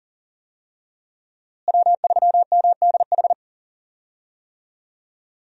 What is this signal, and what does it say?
Morse code, heard as a keyed tone, about 40 wpm, 700 Hz. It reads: W3MDH
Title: USA Call Signs - 5 Characters - Repeated 3x - 40wpm